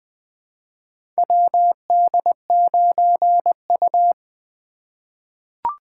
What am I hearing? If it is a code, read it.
Morse code, heard as a keyed tone, about 20 wpm, 700 Hz. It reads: WD9U
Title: USA Call Signs - 3x Character Spacing - 20wpm